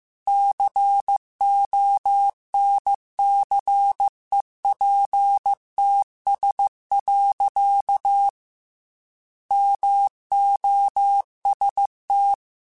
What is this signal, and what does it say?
Morse code, heard as a keyed tone, about 15 wpm, 800 Hz. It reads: CONCEPTS. MOST
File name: Technical Subjects / Technical Subjects-mp3s / Gates Intro @13